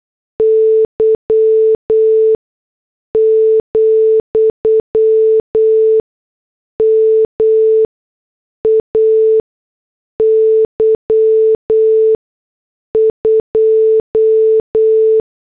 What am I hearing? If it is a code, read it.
Morse code, heard as a keyed tone, about 8 wpm, spaced as for 6 wpm, 440 Hz. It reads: Y, MAY 2